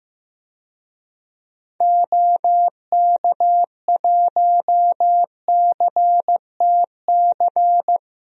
Morse code, heard as a keyed tone, about 15 wpm, 700 Hz. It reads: OK1CTC
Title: DXCC-100 Call Signs - 6 Characters - 15wpm